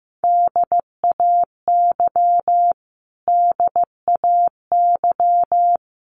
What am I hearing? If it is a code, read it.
Morse code, heard as a keyed tone, about 15 wpm, 700 Hz. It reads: DAY DAY
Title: Top 500 Words Repeated 15wpm